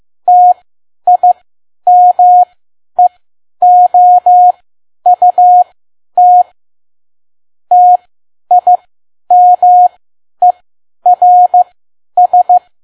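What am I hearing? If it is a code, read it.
Morse code, heard as a keyed tone, about 15 wpm, spaced as for 10 wpm, 720 Hz. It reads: TIMEOUT TIMERS
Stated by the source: Morse code practice at 10 WPM.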